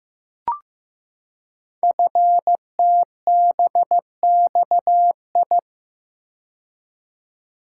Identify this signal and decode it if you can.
Morse code, heard as a keyed tone, about 15 wpm, 700 Hz. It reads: FTBXI
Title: Code Groups - 5 Letters - 15wpm